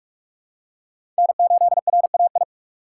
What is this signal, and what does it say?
Morse code, heard as a keyed tone, about 46 wpm, 700 Hz. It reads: N8RAI
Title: USA Call Signs - 5 Characters - 2x Character Spacing - 45wpm